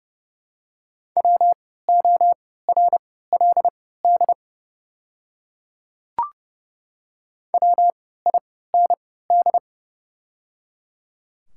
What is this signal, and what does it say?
Morse code, heard as a keyed tone, about 30 wpm, spaced as for 17 wpm, 700 Hz. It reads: WORLD WIND
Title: 3x Character Spacing - Top 500 Words - 30wpm